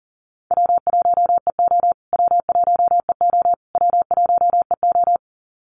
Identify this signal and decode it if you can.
Morse code, heard as a keyed tone, about 40 wpm, 700 Hz. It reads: W1EO W1EO W1EO
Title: USA Call Signs Repeated 3x 40wpm